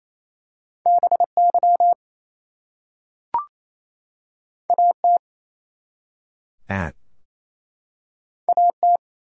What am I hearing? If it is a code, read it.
Morse code, heard as a keyed tone, about 28 wpm, 700 Hz. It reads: BY AT AT
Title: Two Letter Words - 28wpm